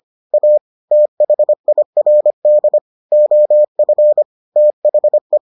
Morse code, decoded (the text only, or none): A THIRD OF THE